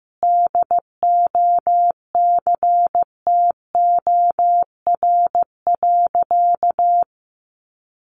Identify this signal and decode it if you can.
Morse code, heard as a keyed tone, about 15 wpm, 700 Hz. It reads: DOCTOR.
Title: Sentences from Top 1500 Words 15wpm